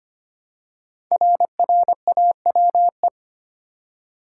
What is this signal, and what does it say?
Morse code, heard as a keyed tone, about 25 wpm, 700 Hz. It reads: RRAWE
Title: Code Groups - 5 Letters - 25wpm